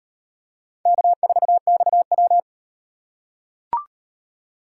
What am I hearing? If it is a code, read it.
Morse code, heard as a keyed tone, about 36 wpm, 700 Hz. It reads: K4XW